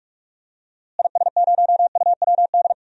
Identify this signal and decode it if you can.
Morse code, heard as a keyed tone, about 44 wpm, 700 Hz. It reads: IS0UWD
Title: DXCC-100 Call Signs - 6 Characters - Repeated 2x - 45wpm